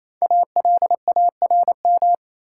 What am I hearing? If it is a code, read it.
Morse code, heard as a keyed tone, about 28 wpm, 700 Hz. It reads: ALARM
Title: Five Letter Words - 28wpm